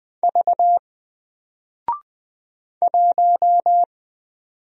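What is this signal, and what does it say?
Morse code, heard as a keyed tone, about 20 wpm, 700 Hz. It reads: V 1
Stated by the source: Single letter and number Morse drill at 20 wpm, from N4LCD.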